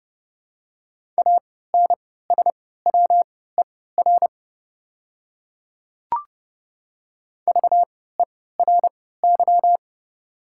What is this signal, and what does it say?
Morse code, heard as a keyed tone, about 31 wpm, spaced as for 17 wpm, 700 Hz. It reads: ANSWER VERY
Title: 3x Character Spacing - Top 500 Words - 30wpm